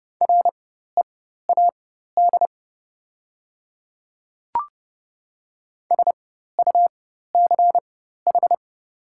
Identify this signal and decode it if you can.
Morse code, heard as a keyed tone, about 29 wpm, spaced as for 14 wpm, 700 Hz. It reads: READ SUCH
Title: 4x Character Spacing - Top 200 Words - 30wpm